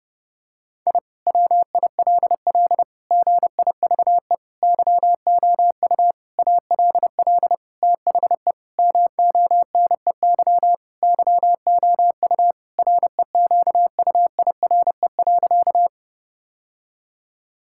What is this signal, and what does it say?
Morse code, heard as a keyed tone, about 28 wpm, 700 Hz. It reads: I WILL GIVE YOU ALL THE MONEY YOU REQUIRE.
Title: Sentences from Top 1000 Words - Review Part 6 - 30wpm